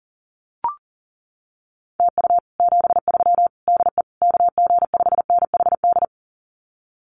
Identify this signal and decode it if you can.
Morse code, heard as a keyed tone, about 37 wpm, 700 Hz. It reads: TU 73 DE KG5NHD
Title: Common QSO Phrases Encore 40wpm